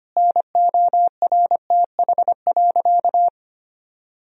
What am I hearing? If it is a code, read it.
Morse code, heard as a keyed tone, about 25 wpm, 700 Hz. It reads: NORTH.
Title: Sentences from Top 1000 Words - Review Part 1 - 25wpm